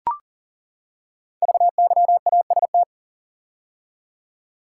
Morse code, heard as a keyed tone, about 42 wpm, 700 Hz. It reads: VYAST
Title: Code Groups - 5 Letters - 40wpm